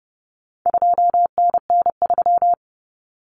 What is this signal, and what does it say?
Morse code, heard as a keyed tone, about 30 wpm, 700 Hz. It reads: 2NN3
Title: Common Cut Numbers - 4-Digit 30wpm